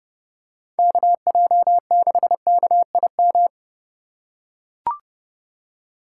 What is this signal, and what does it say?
Morse code, heard as a keyed tone, about 30 wpm, 700 Hz. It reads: KJ6KIM